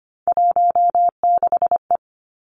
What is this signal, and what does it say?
Morse code, heard as a keyed tone, about 25 wpm, 700 Hz. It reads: 16E
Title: Three Challenging Letter-Number Combinations 25wpm